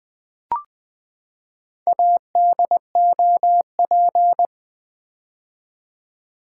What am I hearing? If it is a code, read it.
Morse code, heard as a keyed tone, about 20 wpm, 700 Hz. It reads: ADOP
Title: Build A Word - 4 Letters At A Time - 8-Letter Words - 20wpm